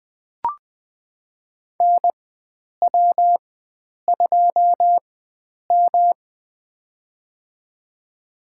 Morse code, heard as a keyed tone, about 21 wpm, spaced as for 9 wpm, 700 Hz. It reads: NW2M
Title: USA Call Signs - 4 Characters - 4x Character Spacing - 20wpm